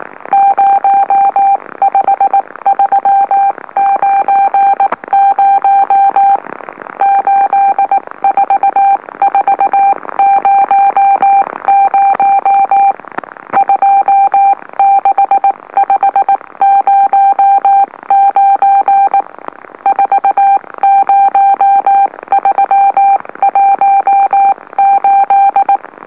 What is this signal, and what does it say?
Morse code, heard as a keyed tone, about 19 wpm, spaced as for 16 wpm, 790 Hz. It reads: 05390 84400 26509 40318
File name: M04- -2345utc-LOLO